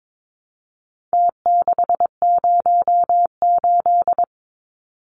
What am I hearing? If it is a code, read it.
Morse code, heard as a keyed tone, about 22 wpm, 700 Hz. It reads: T608
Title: Numbers - Contest Style - 4-Digit 22wpm